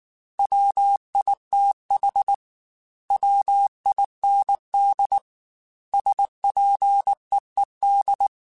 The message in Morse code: WITH WIND SPEED

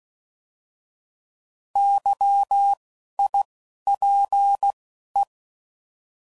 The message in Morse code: YIPE

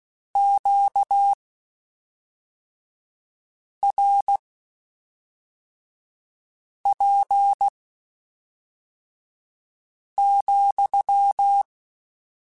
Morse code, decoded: QRP,

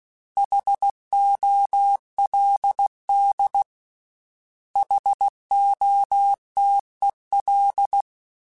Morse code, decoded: HOLD HOTEL

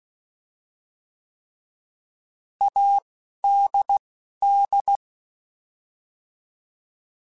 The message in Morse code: ADD

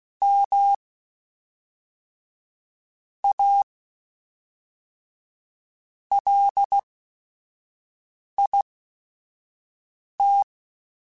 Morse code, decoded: M A LIT

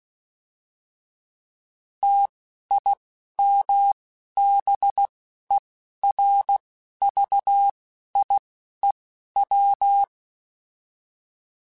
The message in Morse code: TIMBERVIEW

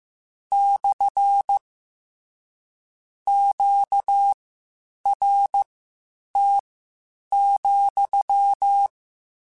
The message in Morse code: / QRT,

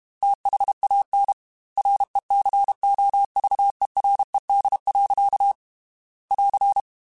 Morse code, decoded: THAN RECOVERED. <AR>

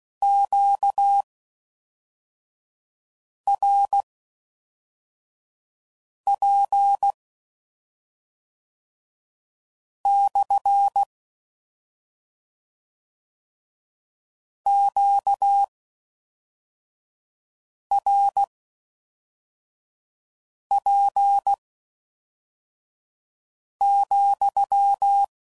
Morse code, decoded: QRP/ QRP,

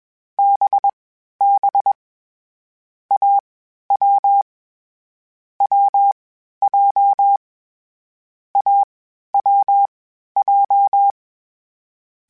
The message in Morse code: BB AW WJ AWJ

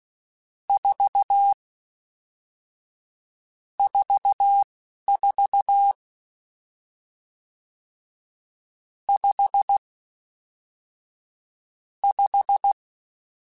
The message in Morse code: 4 44 5 5

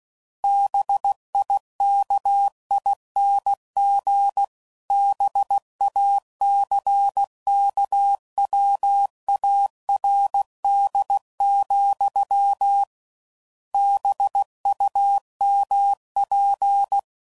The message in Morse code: BIKING BACKWARD, BUMP

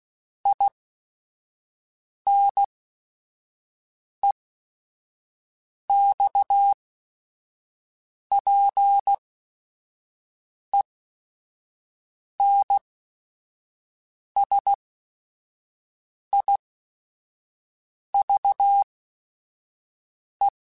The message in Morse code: INEXPENSIVE

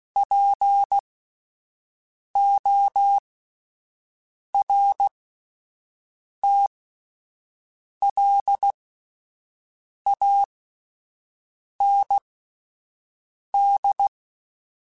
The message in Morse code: PORTLAND

